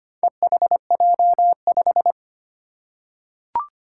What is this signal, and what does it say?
Morse code, heard as a keyed tone, about 25 wpm, 700 Hz. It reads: EHJ5
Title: Four Challenging Letter-Number Combinations 25wpm